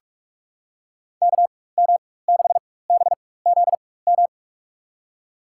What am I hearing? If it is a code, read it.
Morse code, heard as a keyed tone, about 46 wpm, spaced as for 21 wpm, 700 Hz. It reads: KM6BZM
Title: USA Call Signs - 6 Characters - 4x Character Spacing - 45wpm